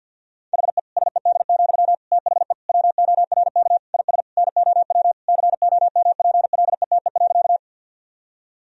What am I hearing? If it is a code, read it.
Morse code, heard as a keyed tone, about 51 wpm, 700 Hz. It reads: HE SED, THE WORK IS NOW COMPLETE.
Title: Sentences from Top 400 Words and Abbreviations 50wpm